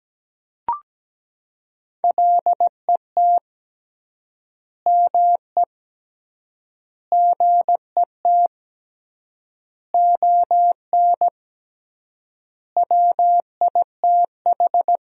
Morse code, heard as a keyed tone, about 17 wpm, 700 Hz. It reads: LET ME GET ON WITH
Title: 3x Word-Spacing - Sentences from Top 300 Words - 17wpm